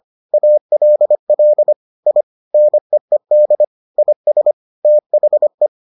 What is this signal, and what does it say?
Morse code, heard as a keyed tone, about 25 wpm, 600 Hz. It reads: ALL I NEED IS THE